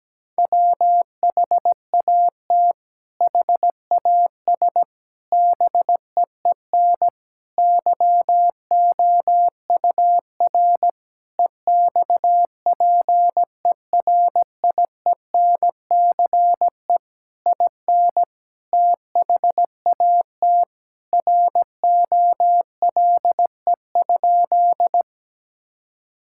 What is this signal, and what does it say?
Morse code, heard as a keyed tone, about 17 wpm, 700 Hz. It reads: WHAT HAS BEEN YOUR EXPERIENCE IN THAT ROLE?
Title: Sentences from Top 1000 Words - Review Part 6 - 17wpm